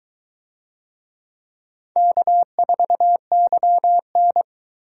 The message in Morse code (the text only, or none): K4YN